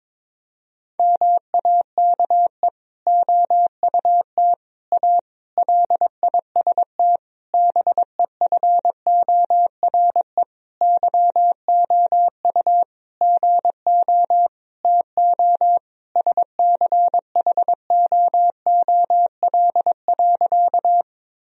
Morse code, heard as a keyed tone, about 21 wpm, 700 Hz. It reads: MAKE OUT A LIST BEFORE YOU GO TO SCHOOL.
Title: Sentences from Top 1000 Words - Review Part 2 - 22wpm